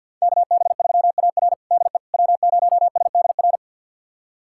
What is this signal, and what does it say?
Morse code, heard as a keyed tone, about 49 wpm, 700 Hz. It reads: KB3AR DE W0SDR
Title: USA Callsign DE Callsign - 5 Characters - 50wpm